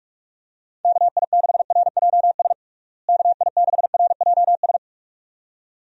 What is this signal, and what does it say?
Morse code, heard as a keyed tone, about 41 wpm, 700 Hz. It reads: KI6RJS KI6RJS
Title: USA Call Signs - 6 Characters - Repeated 2x - 45wpm